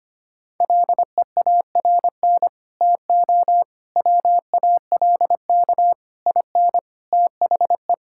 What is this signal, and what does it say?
Morse code, heard as a keyed tone, about 25 wpm, 700 Hz. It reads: LEARN TO WALK IN THE
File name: practice / 25wpm / Sentences with 9-Word Phrases - 25wpm